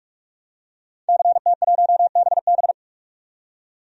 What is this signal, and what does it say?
Morse code, heard as a keyed tone, about 45 wpm, 700 Hz. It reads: CT1BB
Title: DXCC-100 Call Signs - 5 Characters - 45wpm